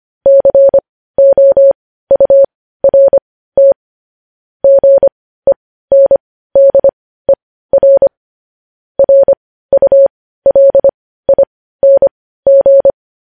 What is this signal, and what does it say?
Morse code, heard as a keyed tone, about 25 wpm, spaced as for 15 wpm, 550 Hz. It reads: COURT GENDER RULING
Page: news headlines in morse code at 15 wpm